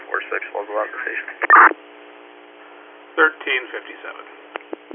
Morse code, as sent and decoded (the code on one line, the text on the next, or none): none